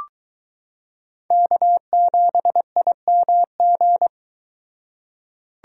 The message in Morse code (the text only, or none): K7IMG